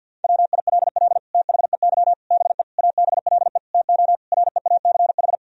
Morse code, heard as a keyed tone, about 52 wpm, 700 Hz. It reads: WILL THEY BE ABLE TO REACH
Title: Sentences from Top 1000 Words - Review Part 2 - 50wpm